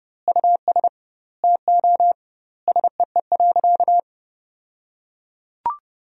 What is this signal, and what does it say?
Morse code, heard as a keyed tone, about 31 wpm, 700 Hz. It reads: US TO SEE.
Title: 2x Word-Spacing - Sentences from Top 500 Words - 30wpm